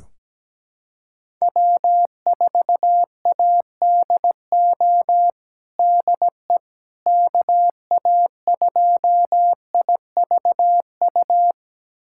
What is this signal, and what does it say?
Morse code, heard as a keyed tone, about 17 wpm, 700 Hz. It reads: W4ADO DE KA2IVU
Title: USA Callsign DE Callsign 17wpm